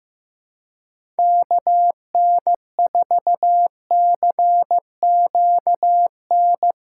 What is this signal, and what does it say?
Morse code, heard as a keyed tone, about 15 wpm, 700 Hz. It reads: KN4CQN
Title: USA Callsign DE Callsign - 6 Characters - 15wpm